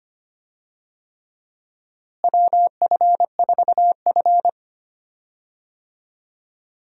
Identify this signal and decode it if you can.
Morse code, heard as a keyed tone, about 25 wpm, 700 Hz. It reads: WF4F
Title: USA Call Signs - 4 Characters - Repeated 3x - 25wpm